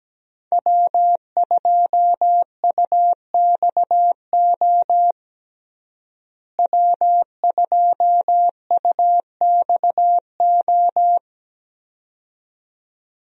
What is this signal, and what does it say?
Morse code, heard as a keyed tone, about 17 wpm, 700 Hz. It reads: W2UXO W2UXO
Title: USA Call Signs - 5 Characters - Repeated 2x - 17wpm